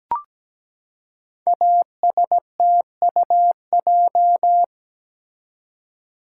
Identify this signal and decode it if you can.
Morse code, heard as a keyed tone, about 17 wpm, 700 Hz. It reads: ASTUJ
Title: Code Groups - 5 Letters - 17wpm